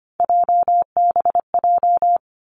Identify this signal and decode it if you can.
Morse code, heard as a keyed tone, about 25 wpm, 700 Hz. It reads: JBJ
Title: Three Challenging Letter-Number Combinations 25wpm